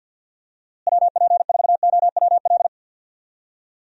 Morse code, heard as a keyed tone, about 52 wpm, 700 Hz. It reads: WP4OWL
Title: USA Call Signs - 6 Characters - Repeated 3x - 50wpm